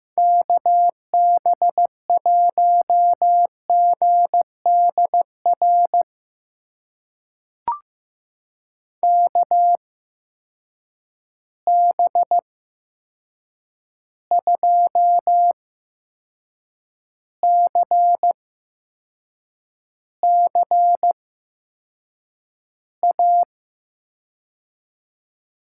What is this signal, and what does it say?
Morse code, heard as a keyed tone, about 15 wpm, spaced as for 6 wpm, 700 Hz. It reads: KB1GDR KB2CCA
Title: USA Call Signs - 6 Characters - 8x Character Spacing - 15wpm